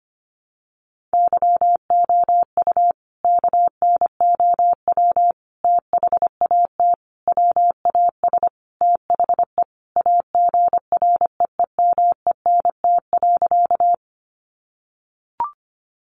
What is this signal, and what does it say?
Morse code, heard as a keyed tone, about 25 wpm, 700 Hz. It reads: YOU KNOW THAT WAS THE AGREEMENT.
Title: Sentences from Top 1500 Words 25wpm